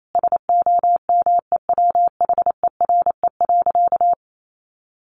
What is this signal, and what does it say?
Morse code, heard as a keyed tone, about 28 wpm, 700 Hz. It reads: SOMEWHERE.
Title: Sentences from Top 1400 Words 28wpm